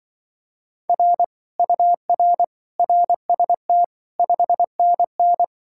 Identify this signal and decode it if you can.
Morse code, heard as a keyed tone, about 24 wpm, 700 Hz. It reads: R UR RST 5NN
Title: Speed-Racing Common QSO Phrases Encore - 20wpm